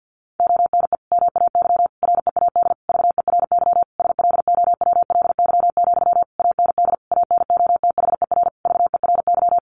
Morse code, heard as a keyed tone, about 50 wpm, 700 Hz. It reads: ONE MAY READ VERY SLOWLY, AND ANOTHER VERY
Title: Sentences from Top 1100 Words 50wpm